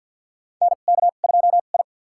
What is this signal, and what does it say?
Morse code, heard as a keyed tone, about 52 wpm, spaced as for 36 wpm, 700 Hz. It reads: NK2I
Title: USA Call Signs - 2x Character Spacing - 50wpm